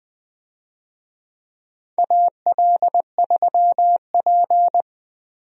Morse code, heard as a keyed tone, about 20 wpm, 700 Hz. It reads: AL3P
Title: USA Call Signs - 4 Characters - Repeated 3x - 20wpm